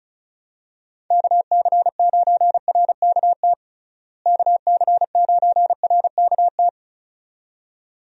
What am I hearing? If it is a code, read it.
Morse code, heard as a keyed tone, about 36 wpm, 700 Hz. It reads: KC9RKT KC9RKT
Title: USA Call Signs - 6 Characters - Repeated 2x - 35wpm